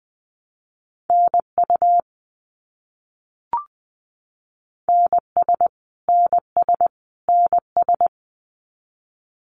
Morse code, heard as a keyed tone, about 20 wpm, 700 Hz. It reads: NU NS NS NS
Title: Canadian Provinces and Territories - Repeated 3x - 20wpm